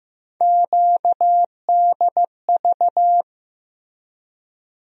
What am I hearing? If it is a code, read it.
Morse code, heard as a keyed tone, about 15 wpm, 700 Hz. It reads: QDV
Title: USA Call Sign Postfixes - 3 Letters 15wpm